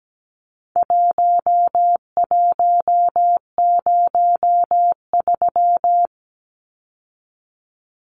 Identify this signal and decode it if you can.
Morse code, heard as a keyed tone, about 17 wpm, 700 Hz. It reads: 1103
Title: Numbers - Contest Style - 4-Digit 17wpm